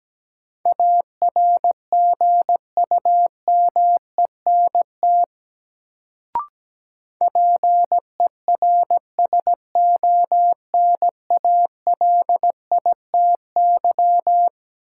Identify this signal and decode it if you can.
Morse code, heard as a keyed tone, about 17 wpm, 700 Hz. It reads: ARGUMENT PERSONALITY
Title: Sentences from Top 1400 Words - New Word Drill - 17wpm